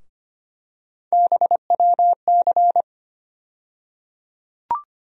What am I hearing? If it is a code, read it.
Morse code, heard as a keyed tone, about 25 wpm, 700 Hz. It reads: BWC
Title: USA Call Sign Postfixes - 3 Letters 25wpm